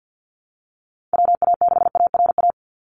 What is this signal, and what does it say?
Morse code, heard as a keyed tone, about 50 wpm, 700 Hz. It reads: WA6ARA
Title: NA SOTA Chaser Callsigns 50wpm